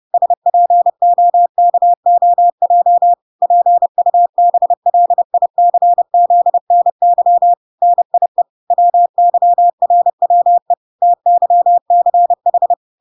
Morse code, heard as a keyed tone, about 30 wpm, 700 Hz. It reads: SPOKOJ PUBLICZNY NIE WYRWE TYCH